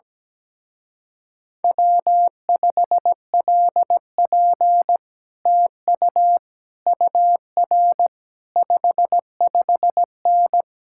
Morse code, heard as a keyed tone, about 17 wpm, 700 Hz. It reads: W5LP TU UR 55N
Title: Speed-Racing Common QSO Phrases Encore - 17wpm